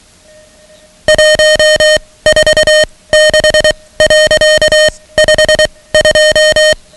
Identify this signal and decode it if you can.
Morse code, heard as a keyed tone, about 23 wpm, spaced as for 17 wpm, 620 Hz, with noise. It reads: 146.52
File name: ham radio / CodeQuick / Code Quick 10